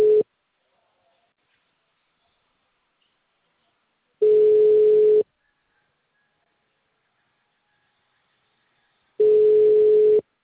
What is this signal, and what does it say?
Ringback tone (Europe, 425 Hz).